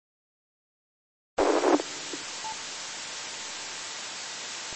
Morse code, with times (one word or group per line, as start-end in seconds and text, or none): none